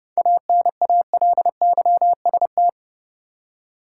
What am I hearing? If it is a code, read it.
Morse code, heard as a keyed tone, about 30 wpm, 700 Hz. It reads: ANALYST
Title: Seven Letter Words - 30wpm